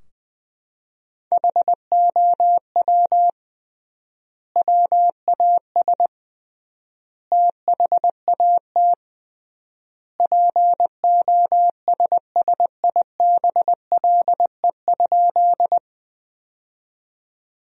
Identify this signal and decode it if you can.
Morse code, heard as a keyed tone, about 20 wpm, 700 Hz. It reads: HOW WAS THAT POSSIBLE?
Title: 3x Word-Spacing - Sentences from Top 500 Words - 20wpm